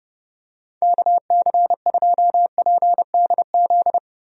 Morse code, heard as a keyed tone, about 30 wpm, 700 Hz. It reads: KC2PDZ
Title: USA Call Signs - 6 Characters - Repeated 3x - 30wpm